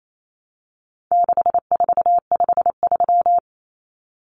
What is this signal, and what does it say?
Morse code, heard as a keyed tone, about 28 wpm, 700 Hz. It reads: 6453
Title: Numbers - 4-Digit 28wpm